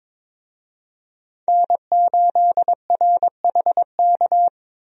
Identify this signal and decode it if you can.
Morse code, heard as a keyed tone, about 22 wpm, 700 Hz. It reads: N8RHK
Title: USA Call Signs - 5 Characters - Repeated 3x - 22wpm